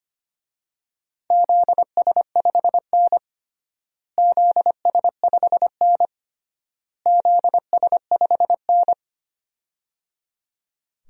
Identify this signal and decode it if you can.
Morse code, heard as a keyed tone, about 25 wpm, 700 Hz. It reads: ZS5N ZS5N ZS5N
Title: DXCC-100 Call Signs - 4 Characters - Repeated 3x - 25wpm